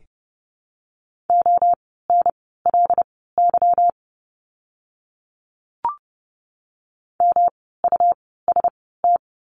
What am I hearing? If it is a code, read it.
Morse code, heard as a keyed tone, about 30 wpm, spaced as for 17 wpm, 700 Hz. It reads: ONLY MUST
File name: course / 30:10wpm / 237 Lesson 38f - Letter q - Words 30wpm